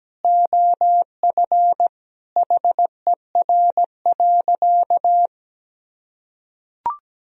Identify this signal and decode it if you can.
Morse code, heard as a keyed tone, about 17 wpm, 700 Hz. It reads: OF HER.
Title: Sentences from Top 1000 Words - Review Part 4 - 17wpm